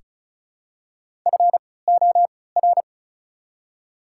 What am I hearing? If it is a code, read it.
Morse code, heard as a keyed tone, about 36 wpm, spaced as for 20 wpm, 700 Hz. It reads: FOR
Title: 3x Character Spacing - Top 100 Words - 35wpm